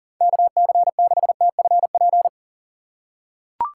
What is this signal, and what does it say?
Morse code, heard as a keyed tone, about 40 wpm, 700 Hz. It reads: KC6TFP
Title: USA Call Signs - 6 Characters - 4x Character Spacing - 40wpm